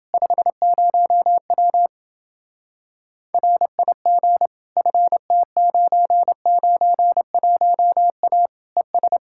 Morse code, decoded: 50W RIG FT991A ES

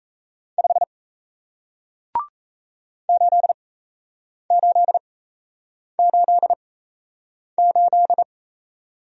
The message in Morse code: 5 8888

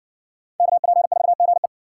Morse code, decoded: KP4GE

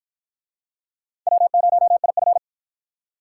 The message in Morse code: W0IF

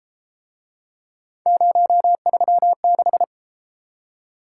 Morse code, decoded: 036

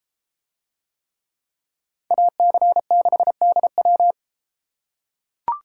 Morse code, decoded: AC6DW

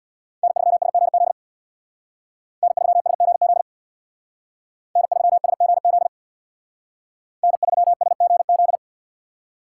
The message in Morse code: N3SGZ N3SGZ N3SGZ N3SGZ

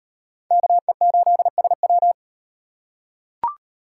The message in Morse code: KE8SW

